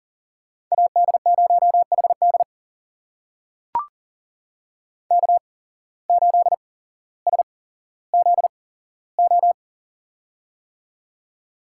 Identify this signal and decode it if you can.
Morse code, heard as a keyed tone, about 40 wpm, spaced as for 17 wpm, 700 Hz. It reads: AD0HD K8SZO
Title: USA Call Signs - 5 Characters - 8x Character Spacing - 40wpm